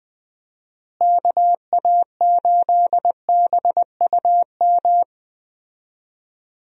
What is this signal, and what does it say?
Morse code, heard as a keyed tone, about 20 wpm, 700 Hz. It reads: KA8BUM
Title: USA Call Signs - 6 Characters - Repeated 3x - 20wpm